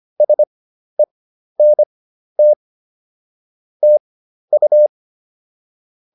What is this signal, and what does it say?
Morse code, heard as a keyed tone, about 25 wpm, spaced as for 12 wpm, 600 Hz. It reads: SENT TU